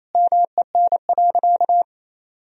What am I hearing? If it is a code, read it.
Morse code, heard as a keyed tone, about 27 wpm, 700 Hz. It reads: MEN.
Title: Sentences from Top 1000 Words - Review Part 3 - 28wpm